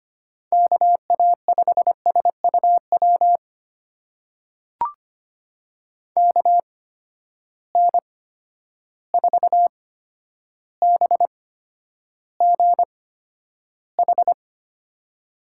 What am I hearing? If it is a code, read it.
Morse code, heard as a keyed tone, about 25 wpm, spaced as for 11 wpm, 700 Hz. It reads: KA5SUW KN4BGH
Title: USA Call Signs - 6 Characters - 8x Character Spacing - 25wpm